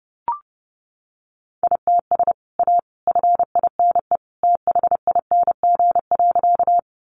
Morse code, heard as a keyed tone, about 30 wpm, 700 Hz. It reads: ITS A FINE THING.
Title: Sentences from Top 600 Words 30wpm